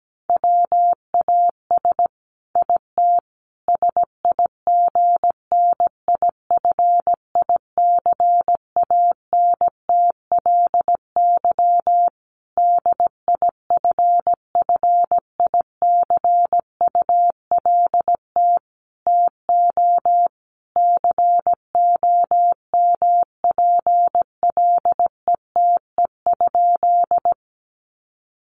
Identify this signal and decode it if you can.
Morse code, heard as a keyed tone, about 17 wpm, 700 Hz. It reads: WAS IT SIGNIFICANTLY DIFFICULT TO COMPLETE?
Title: Sentences from Top 1500 Words 17wpm